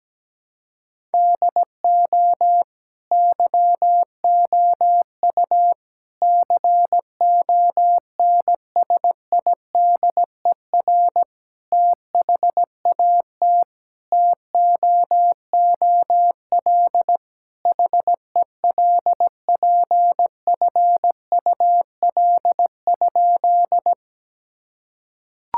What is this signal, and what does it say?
Morse code, heard as a keyed tone, about 17 wpm, 700 Hz. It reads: DO YOU CONSIDER THAT TOOL HELPFUL?
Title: Sentences from Top 1000 Words - Review Part 6 - 17wpm